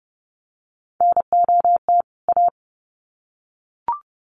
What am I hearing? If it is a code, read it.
Morse code, heard as a keyed tone, about 30 wpm, 700 Hz. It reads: NOT A
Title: Sets of 2 Words 30wpm